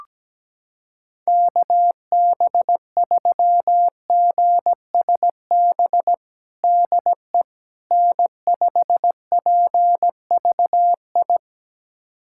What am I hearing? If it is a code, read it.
Morse code, heard as a keyed tone, about 17 wpm, 700 Hz. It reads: KB3GSB DE N5PVI
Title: USA Callsign DE Callsign 17wpm